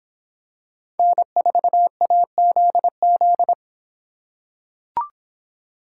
Morse code, heard as a keyed tone, about 26 wpm, 700 Hz. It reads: N4AZZ